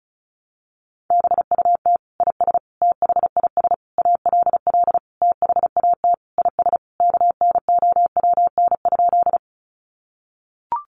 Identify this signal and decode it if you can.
Morse code, heard as a keyed tone, about 34 wpm, 700 Hz. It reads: BUT IS THIS ALL THAT IS KNOWN?
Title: Sentences from Top 600 Words 35wpm